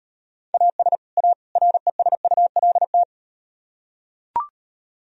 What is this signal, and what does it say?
Morse code, heard as a keyed tone, about 38 wpm, 700 Hz. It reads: AS A RESULT